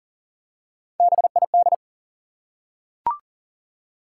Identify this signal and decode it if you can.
Morse code, heard as a keyed tone, about 40 wpm, 700 Hz. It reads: BID
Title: Three Letter Words - 40wpm